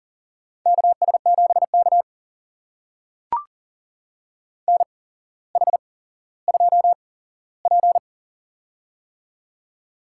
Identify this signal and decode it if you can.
Morse code, heard as a keyed tone, about 40 wpm, spaced as for 17 wpm, 700 Hz. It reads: KS7K NH2P
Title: USA Call Signs - 8x Character Spacing - 40wpm